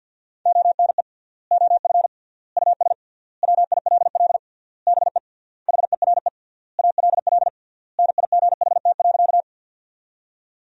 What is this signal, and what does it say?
Morse code, heard as a keyed tone, about 51 wpm, 700 Hz. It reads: ONE OF US WILL BE HERE ALL NIGHT.
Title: 3x Word-Spacing - Sentences from Top 300 Words - 50wpm